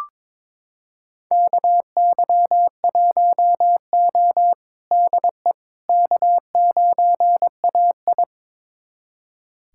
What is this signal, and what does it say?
Morse code, heard as a keyed tone, about 22 wpm, 700 Hz. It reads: KY1O DE K9AI
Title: USA Callsign DE Callsign - 4 Characters - 22wpm